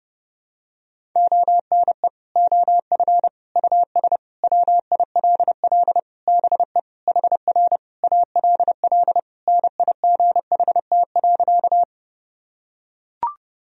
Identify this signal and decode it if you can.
Morse code, heard as a keyed tone, about 30 wpm, 700 Hz. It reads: ONE OF US WILL BE HR ALL NIGHT.
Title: Sentences from Top 300 Words and Abbreviations 30wpm